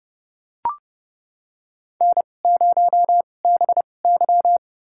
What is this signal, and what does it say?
Morse code, heard as a keyed tone, about 30 wpm, spaced as for 22 wpm, 700 Hz. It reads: N0BY
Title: USA Call Signs - 4 Characters - 2x Character Spacing - 30wpm